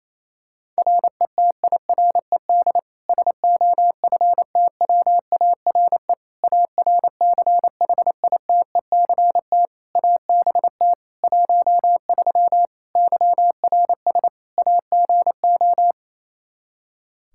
Morse code, heard as a keyed tone, about 28 wpm, 700 Hz. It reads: RETIRED SOFTWARE ARCHITECT ABT 13 YRS AGO
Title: Practice QSO - Exchange 3 - More Info - 28wpm